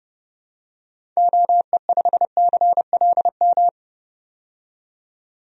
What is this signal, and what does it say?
Morse code, heard as a keyed tone, about 30 wpm, 700 Hz. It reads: OE5CLM
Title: DXCC-100 Call Signs - 6 Characters - 30wpm